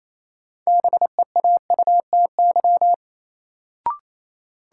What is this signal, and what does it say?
Morse code, heard as a keyed tone, about 28 wpm, 700 Hz. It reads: BEAUTY
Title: Sentences from Top 600 Words - New Word Drill - 28wpm